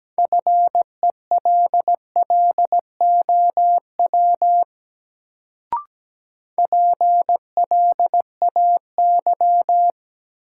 Morse code, heard as a keyed tone, about 17 wpm, 700 Hz. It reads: FELLOW PLAY
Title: Sets of 5 Words Encore - New Word Drill - 17wpm